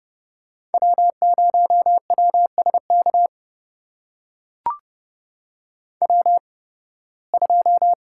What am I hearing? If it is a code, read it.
Morse code, heard as a keyed tone, about 30 wpm, spaced as for 20 wpm, 700 Hz. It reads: W0WSK W2
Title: USA Call Signs - 5 Characters - 8x Character Spacing - 30wpm